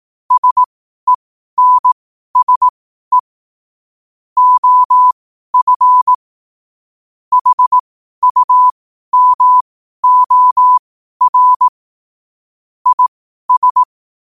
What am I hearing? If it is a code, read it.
Morse code, heard as a keyed tone, about 18 wpm, spaced as for 12 wpm, 1000 Hz. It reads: SENSE OF HUMOR IS
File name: qotdfiles / 12wpm